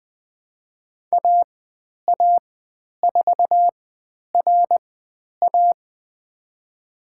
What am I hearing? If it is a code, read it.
Morse code, heard as a keyed tone, about 20 wpm, spaced as for 10 wpm, 700 Hz. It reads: AA4RA
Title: 102 Lesson 17g - Letter u - Call Signs 20wpm